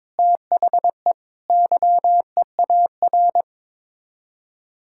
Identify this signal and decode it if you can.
Morse code, heard as a keyed tone, about 22 wpm, 700 Hz. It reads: THE YEAR
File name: practice / 22wpm / Sets of 7 Words - 22wpm